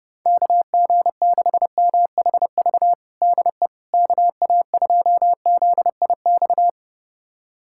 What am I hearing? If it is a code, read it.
Morse code, heard as a keyed tone, about 30 wpm, 700 Hz. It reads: KG6MHV DE KA2ZIX
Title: USA Callsign DE Callsign 30wpm